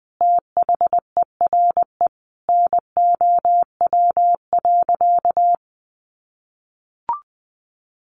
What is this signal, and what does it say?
Morse code, heard as a keyed tone, about 20 wpm, 700 Hz. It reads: THERE NOW.
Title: Sentences from Top 200 Words 20wpm